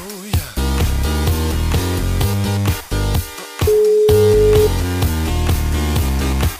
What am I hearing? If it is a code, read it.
Ringback tone (Europe, 425 Hz), with noise.